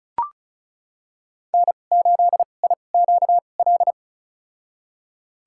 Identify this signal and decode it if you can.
Morse code, heard as a keyed tone, about 36 wpm, spaced as for 25 wpm, 700 Hz. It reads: N8IQL
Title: USA Call Signs - 2x Character Spacing - 35wpm